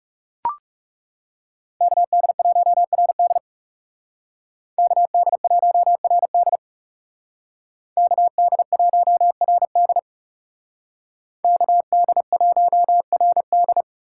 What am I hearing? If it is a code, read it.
Morse code, heard as a keyed tone, about 38 wpm, 700 Hz. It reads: KD1RD KD1RD KD1RD KD1RD